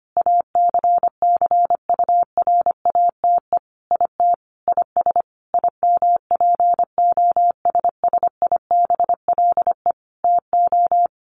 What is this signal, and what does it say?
Morse code, heard as a keyed tone, about 25 wpm, 700 Hz. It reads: ACCURATE IT IS IMPOSSIBLE TO
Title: Sentences from Top 1200 Words 25wpm